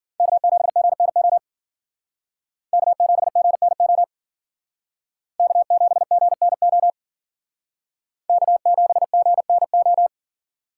Morse code, heard as a keyed tone, about 51 wpm, 700 Hz. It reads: K7GNO K7GNO K7GNO K7GNO